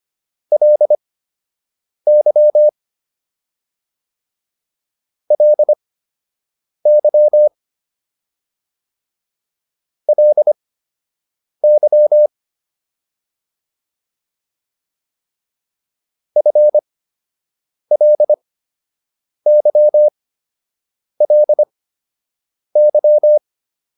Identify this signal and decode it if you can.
Morse code, heard as a keyed tone, about 25 wpm, spaced as for 7 wpm, 600 Hz. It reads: LY LY LY FLYLY